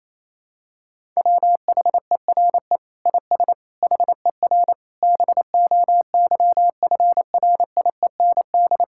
Morse code, heard as a keyed tone, about 27 wpm, 700 Hz. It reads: WHERE IS HER BOYFRIEND
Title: Sentences from Top 1000 Words - Review Part 5 - 28wpm